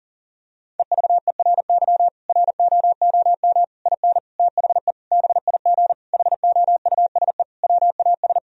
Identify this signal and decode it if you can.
Morse code, heard as a keyed tone, about 40 wpm, 700 Hz. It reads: EVERY ROOM IN THE BIG HOUSE WAS
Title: Sentences from Top 1000 Words - Review Part 3 - 40wpm